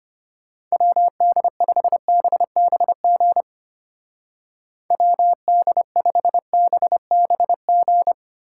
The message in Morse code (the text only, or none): WD5BBG WD5BBG